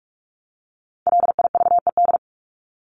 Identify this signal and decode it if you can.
Morse code, heard as a keyed tone, about 44 wpm, 700 Hz. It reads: LIVED
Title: ED Suffix - Top 100 Words with Base Words 45wpm